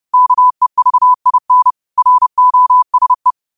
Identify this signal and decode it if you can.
Morse code, heard as a keyed tone, about 30 wpm, 1000 Hz. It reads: KEVIN ROSE
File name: qotdfiles / 30wpm